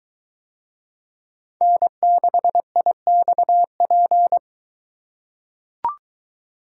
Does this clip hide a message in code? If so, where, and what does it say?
Yes: Morse code from 1.6 to 4.4 s, heard as a keyed tone, about 22 wpm, 700 Hz. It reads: N6IXP